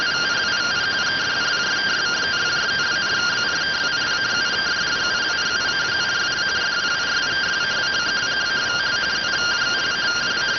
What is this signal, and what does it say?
Two-tone teleprinter signal, about 170 Hz shift.